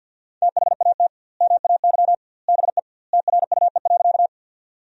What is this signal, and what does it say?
Morse code, heard as a keyed tone, about 50 wpm, 700 Hz. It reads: THAT MAY BE TRUE.
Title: 2x Word-Spacing - Sentences from Top 400 Words - 50wpm